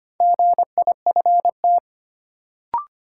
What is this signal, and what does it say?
Morse code, heard as a keyed tone, about 25 wpm, 700 Hz. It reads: GIFT
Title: Sentences from Top 900 Words - New Word Drill - 25wpm